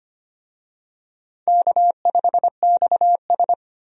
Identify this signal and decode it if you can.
Morse code, heard as a keyed tone, about 25 wpm, 700 Hz. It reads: K5XS
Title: USA Call Signs - 4 Characters - 25wpm